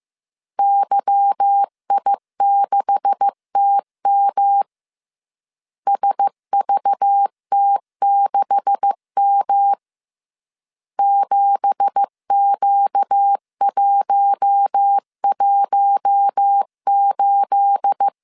Morse code, heard as a keyed tone, about 15 wpm, 800 Hz. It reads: YI6TM SVT6M 7Q118
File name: CW / DET / 1 9 1500C S C 14wpm 3245